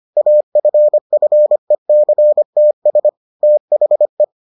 Morse code, heard as a keyed tone, about 25 wpm, 600 Hz. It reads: AFFECTS THE